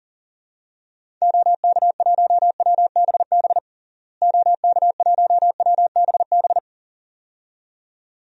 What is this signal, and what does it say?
Morse code, heard as a keyed tone, about 40 wpm, 700 Hz. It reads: OK1WBB OK1WBB
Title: DXCC-100 Call Signs - 6 Characters - Repeated 2x - 40wpm